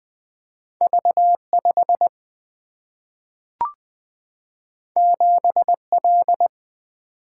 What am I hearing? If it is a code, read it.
Morse code, heard as a keyed tone, about 20 wpm, 700 Hz. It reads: V5 7L
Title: Single-Letter Single-Number 20wpm